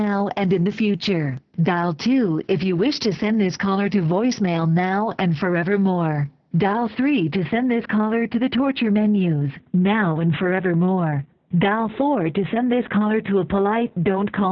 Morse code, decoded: none